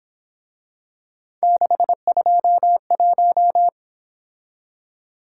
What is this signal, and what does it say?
Morse code, heard as a keyed tone, about 26 wpm, 700 Hz. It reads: 621
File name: Speed-Racing - Three Letter-Number Combinations - 22wpm